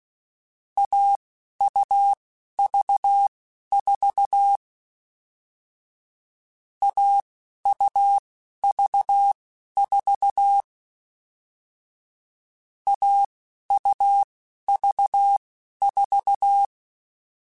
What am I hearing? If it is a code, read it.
Morse code, heard as a keyed tone, about 16 wpm, spaced as for 11 wpm, 800 Hz. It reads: AUV4 AUV4 AUV4
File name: Difficult Combos / Difficult Combos-mp3s / AUV AUV4 @10